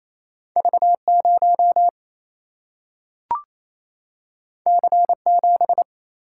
Morse code, heard as a keyed tone, about 28 wpm, 700 Hz. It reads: V0 C7
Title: Single-Letter Single-Number 28wpm